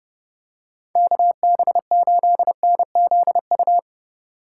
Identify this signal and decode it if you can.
Morse code, heard as a keyed tone, about 30 wpm, 700 Hz. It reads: KB8NZU
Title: USA Call Signs - 6 Characters - 2x Character Spacing - 30wpm